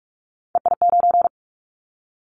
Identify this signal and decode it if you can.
Morse code, heard as a keyed tone, about 46 wpm, 700 Hz. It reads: EI9
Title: Callsign Prefixes for Top 50 DXCC - 45wpm